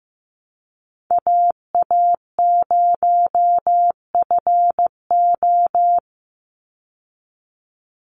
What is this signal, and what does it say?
Morse code, heard as a keyed tone, about 15 wpm, 700 Hz. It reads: AA0FO
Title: Call Signs Repeated 3x - 15wpm